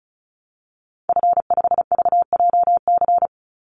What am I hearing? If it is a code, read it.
Morse code, heard as a keyed tone, about 34 wpm, 700 Hz. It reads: F5VJC
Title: Call Signs - 35wpm